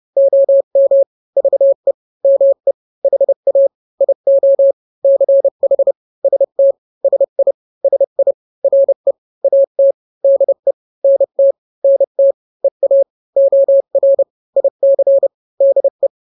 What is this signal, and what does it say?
Morse code, heard as a keyed tone, about 30 wpm, 550 Hz. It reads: OM VE ME HA IO CH ST SI SI RE AT DE NT NT EA OR IC DE
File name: n-grams / 2-grams 30 25 WPM